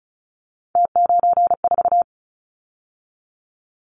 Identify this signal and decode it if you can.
Morse code, heard as a keyed tone, about 34 wpm, 700 Hz. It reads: T94
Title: Callsign Prefixes for Top 100 DXCC - 35wpm